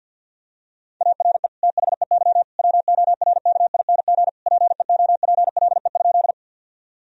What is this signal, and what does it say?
Morse code, heard as a keyed tone, about 50 wpm, 700 Hz. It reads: ARE THEY WORKING PEOPLE?
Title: Sentences from Top 1000 Words - Review Part 4 - 50wpm